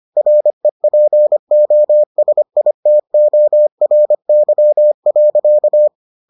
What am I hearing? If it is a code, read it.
Morse code, heard as a keyed tone, about 25 wpm, 600 Hz. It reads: REPOSITORY.